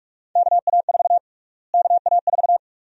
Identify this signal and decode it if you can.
Morse code, heard as a keyed tone, about 46 wpm, 700 Hz. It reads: KA4 KA4
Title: USA Call Sign Prefixes - Repeated 2x - 45wpm